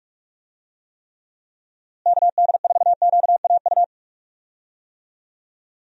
Morse code, heard as a keyed tone, about 44 wpm, 700 Hz. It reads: KD4QAU
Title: USA Call Signs - 6 Characters - Repeated 3x - 45wpm